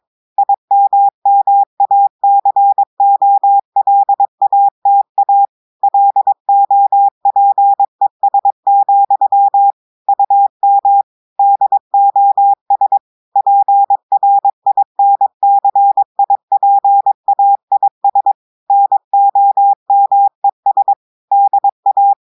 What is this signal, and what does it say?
Morse code, heard as a keyed tone, about 22 wpm, 800 Hz. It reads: IMMACOLATA LOPES, UM DOS PRINCIPAIS NOMES DA
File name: audios / portugais / portugais texte en clair 1 vitesse 22 mots minutes